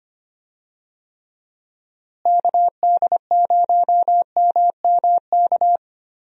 Morse code, heard as a keyed tone, about 25 wpm, 700 Hz. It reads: KD0MMK
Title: USA Call Signs - 6 Characters - Repeated 3x - 25wpm